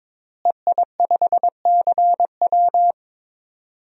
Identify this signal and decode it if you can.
Morse code, heard as a keyed tone, about 21 wpm, 700 Hz. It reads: EI5CW
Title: DXCC-100 Call Signs - 5 Characters - Repeated 2x - 22wpm